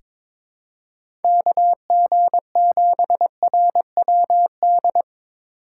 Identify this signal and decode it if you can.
Morse code, heard as a keyed tone, about 22 wpm, 700 Hz. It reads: KG7RWD
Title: USA Call Signs - 6 Characters - 8x Character Spacing - 22wpm